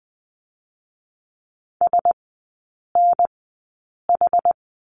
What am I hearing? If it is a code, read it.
Morse code, heard as a keyed tone, about 20 wpm, spaced as for 8 wpm, 700 Hz. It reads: SNH
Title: 127 Lesson 21e - Number 6 - Three Characters 20wpm